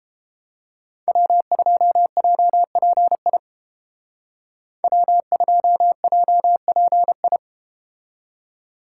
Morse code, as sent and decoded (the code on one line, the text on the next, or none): .-- ..--- .--- .--. .. / .-- ..--- .--- .--. ..
W2JPI W2JPI